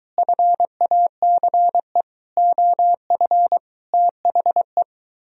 Morse code, heard as a keyed tone, about 23 wpm, 700 Hz. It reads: FACE OF THE